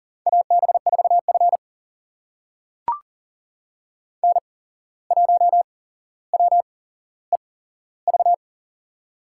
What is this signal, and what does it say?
Morse code, heard as a keyed tone, about 40 wpm, spaced as for 16 wpm, 700 Hz. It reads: AB4F N1WEV
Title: USA Call Signs - 8x Character Spacing - 40wpm